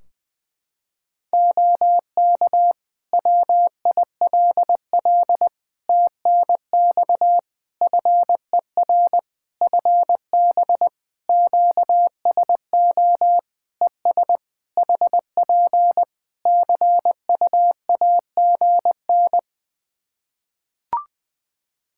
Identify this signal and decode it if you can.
Morse code, heard as a keyed tone, about 20 wpm, 700 Hz. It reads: OK WILL TNX FER FB QSO ES HP CUAGN
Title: Practice QSO - Exchange 4 - Ending - 20wpm